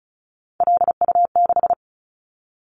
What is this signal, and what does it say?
Morse code, heard as a keyed tone, about 34 wpm, 700 Hz. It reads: LU6
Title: Callsign Prefixes for Top 100 DXCC - 35wpm